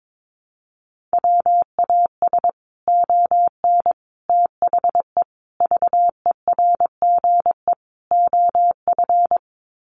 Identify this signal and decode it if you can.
Morse code, heard as a keyed tone, about 22 wpm, 700 Hz. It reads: WAS ON THE VERGE OF
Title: Sets of 5 Words Encore 22wpm